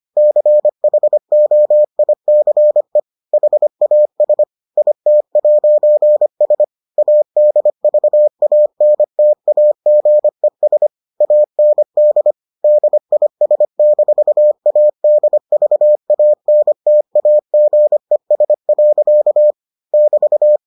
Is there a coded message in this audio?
Yes: Morse code from 0.2 to 20.6 s, heard as a keyed tone, about 25 wpm, 600 Hz. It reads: CHOICE HAS IT'S ADVANTAGES AND DIS-ADVANTAGES. <BT>